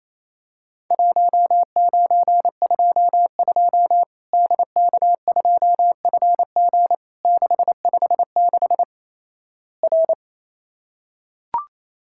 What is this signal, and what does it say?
Morse code, heard as a keyed tone, about 28 wpm, 700 Hz. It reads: 1922 DK2FG 656 R
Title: Worked All Europe DX Contest - QTC Exchange - 28wpm